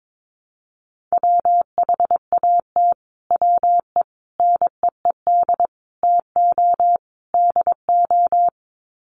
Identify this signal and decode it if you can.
Morse code, heard as a keyed tone, about 22 wpm, 700 Hz. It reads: WHAT WE NEED TO DO
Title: Sets of 5 Words Encore 22wpm